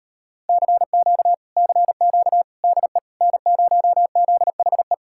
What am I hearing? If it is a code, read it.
Morse code, heard as a keyed tone, about 38 wpm, 700 Hz. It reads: CQ CQ DE N0ZHE